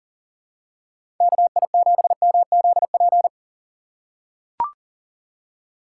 Morse code, heard as a keyed tone, about 40 wpm, 700 Hz. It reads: KI7MZP
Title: USA Call Signs - 6 Characters - 4x Character Spacing - 40wpm